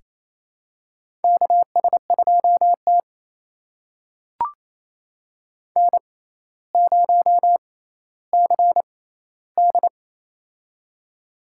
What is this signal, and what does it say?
Morse code, heard as a keyed tone, about 28 wpm, spaced as for 14 wpm, 700 Hz. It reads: KS2T N0CD
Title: USA Call Signs - 4 Characters - 6x Character Spacing - 28wpm